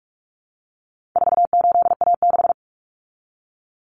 Morse code, heard as a keyed tone, about 44 wpm, 700 Hz. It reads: 48A6
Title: Common Cut Numbers - 4-Digit 45wpm